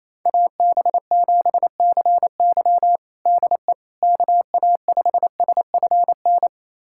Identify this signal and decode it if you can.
Morse code, heard as a keyed tone, about 28 wpm, 700 Hz. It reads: AB7CY DE KA5SFN
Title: USA Callsign DE Callsign 28wpm